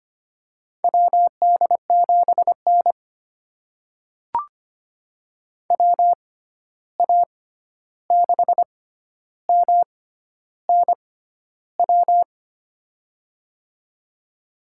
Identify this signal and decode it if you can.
Morse code, heard as a keyed tone, about 25 wpm, spaced as for 11 wpm, 700 Hz. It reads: WD7N WA6MNW
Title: USA Call Signs - 6x Character Spacing - 25wpm